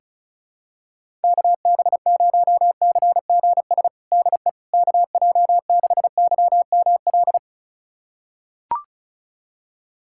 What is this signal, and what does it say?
Morse code, heard as a keyed tone, about 36 wpm, 700 Hz. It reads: KB0CGS DE KJ6YML
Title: USA Callsign DE Callsign - 6 Characters - 35wpm